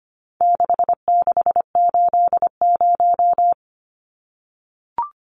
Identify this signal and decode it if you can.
Morse code, heard as a keyed tone, about 25 wpm, 700 Hz. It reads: 6680
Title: Numbers - 4-Digit 25wpm